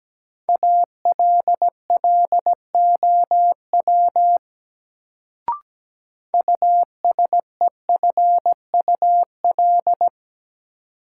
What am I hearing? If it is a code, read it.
Morse code, heard as a keyed tone, about 17 wpm, 700 Hz. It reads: ALLOW USEFUL
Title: Sentences from Top 700 Words - New Word Drill - 17wpm